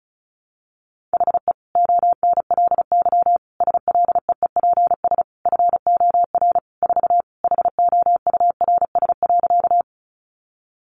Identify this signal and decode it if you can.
Morse code, heard as a keyed tone, about 34 wpm, 700 Hz. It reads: HE ONLY SLEEPS FOR 4 HOURS.
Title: Additional Easy Phrases 35wpm